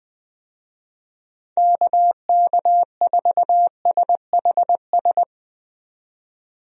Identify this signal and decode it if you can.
Morse code, heard as a keyed tone, about 20 wpm, 700 Hz. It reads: KK4SHS
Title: USA Call Signs - 6 Characters - 20wpm